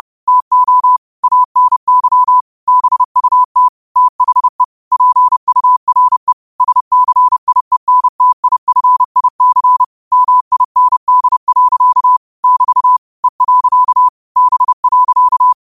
Morse code, heard as a keyed tone, about 30 wpm, 1000 Hz. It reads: TO ANY BUT THE PURE SCIENTIFIC MIND. <BT> E. B.
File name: qotdfiles / 30wpm